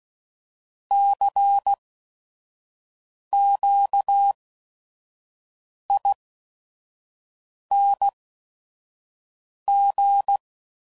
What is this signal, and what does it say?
Morse code, heard as a keyed tone, about 16 wpm, spaced as for 5 wpm, 800 Hz. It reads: CQING
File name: Suffixes / Special ING's / VerySlowToFaster ING's